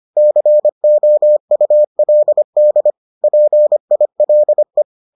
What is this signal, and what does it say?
Morse code, heard as a keyed tone, about 25 wpm, 600 Hz. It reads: COULD PILE